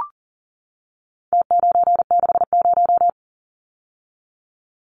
Morse code, heard as a keyed tone, about 40 wpm, 700 Hz. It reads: T960
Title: Numbers - Contest Style - 4-Digit 40wpm